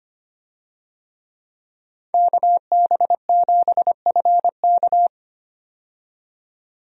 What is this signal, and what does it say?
Morse code, heard as a keyed tone, about 25 wpm, 700 Hz. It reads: KB7FK
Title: USA Call Signs - 5 Characters - 25wpm